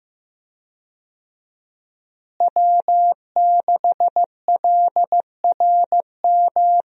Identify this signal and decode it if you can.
Morse code, heard as a keyed tone, about 15 wpm, 700 Hz. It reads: W6LRM
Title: USA Call Signs - 5 Characters - Repeated 3x - 15wpm